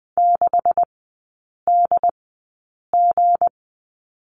Morse code, heard as a keyed tone, about 20 wpm, spaced as for 8 wpm, 700 Hz. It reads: 6DG
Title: 166 Lesson 27e - Number 7 - Three Characters 20wpm